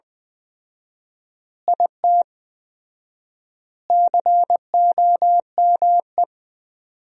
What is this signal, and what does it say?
Morse code, heard as a keyed tone, about 20 wpm, 700 Hz. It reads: IT COME
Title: 4x Word-Spacing - Sentences from Top 100 Words - 20wpm